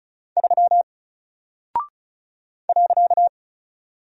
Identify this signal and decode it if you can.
Morse code, heard as a keyed tone, about 34 wpm, 700 Hz. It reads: 3 .
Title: Single Character (Letter:Number:Punctuation:Prosign) Alt 35wpm